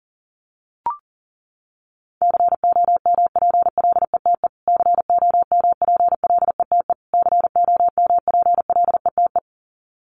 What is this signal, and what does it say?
Morse code, heard as a keyed tone, about 37 wpm, 700 Hz. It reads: COMPLETE COMPLETE COMPLETE
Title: Top 500 Words Repeated 40wpm